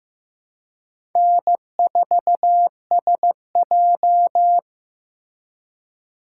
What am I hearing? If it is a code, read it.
Morse code, heard as a keyed tone, about 18 wpm, 700 Hz. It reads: N4SJ
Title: Speed Racing - USA 1x2 Call Signs 15wpm